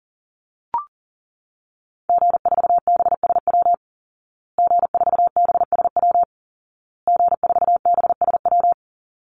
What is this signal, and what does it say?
Morse code, heard as a keyed tone, about 40 wpm, 700 Hz. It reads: G4BSW G4BSW G4BSW
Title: Call Signs Repeated 2x - 40wpm